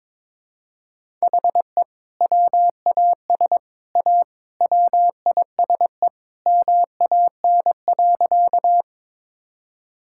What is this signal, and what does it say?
Morse code, heard as a keyed tone, about 22 wpm, 700 Hz. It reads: HE WAS A WISE MAN.
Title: Sentences from Top 1600 Words 22wpm